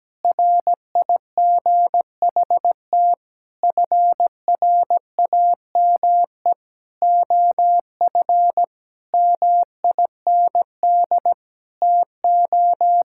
Morse code, heard as a keyed tone, about 17 wpm, 700 Hz. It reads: RIGHT FRAME OF MIND TO
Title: Sentences from Top 1000 Words - Review Part 6 - 17wpm